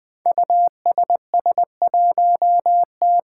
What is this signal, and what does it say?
Morse code, heard as a keyed tone, about 20 wpm, 700 Hz. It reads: USS1T
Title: Five Challenging Letter-Number Combinations 20wpm